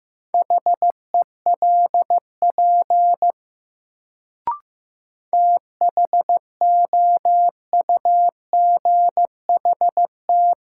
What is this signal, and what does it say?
Morse code, heard as a keyed tone, about 15 wpm, 700 Hz. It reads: HELP THOUGHT
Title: Sets of 2 Words - New Word Drill - 15wpm